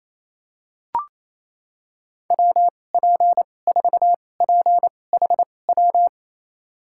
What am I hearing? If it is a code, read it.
Morse code, heard as a keyed tone, about 27 wpm, spaced as for 20 wpm, 700 Hz. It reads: WP4PHW
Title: USA Call Signs - 6 Characters - 2x Character Spacing - 28wpm